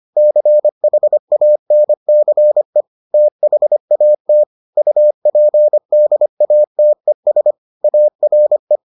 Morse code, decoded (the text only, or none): CHANCE THAT UPDATES ARE